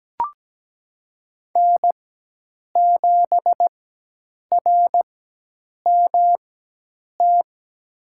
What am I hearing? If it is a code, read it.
Morse code, heard as a keyed tone, about 17 wpm, spaced as for 8 wpm, 700 Hz. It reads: N7RMT
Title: USA Call Signs - 5 Characters - 4x Character Spacing - 17wpm